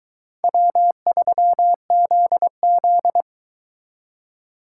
Morse code, heard as a keyed tone, about 22 wpm, 700 Hz. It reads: W3ZZ